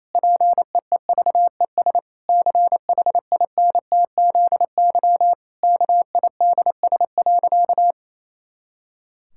Morse code, heard as a keyed tone, about 28 wpm, 700 Hz. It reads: PEEVES CHINTZY KIDS.
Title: Pangram Challenge v2 - 28wpm